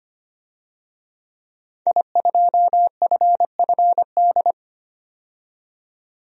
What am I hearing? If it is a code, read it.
Morse code, heard as a keyed tone, about 25 wpm, 700 Hz. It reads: I2FFD
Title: DXCC-100 Call Signs - 5 Characters - 25wpm